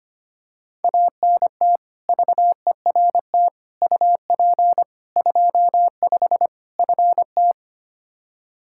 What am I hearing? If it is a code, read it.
Morse code, heard as a keyed tone, about 25 wpm, 700 Hz. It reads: ANT VERT UP 25 FT
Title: K1USN SST QSO Practice 25wpm